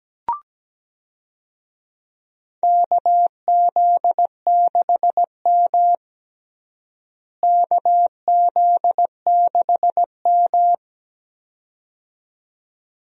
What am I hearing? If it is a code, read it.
Morse code, heard as a keyed tone, about 17 wpm, 700 Hz. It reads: KZ6M KZ6M
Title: USA Call Signs - 4 Characters - Repeated 2x - 17wpm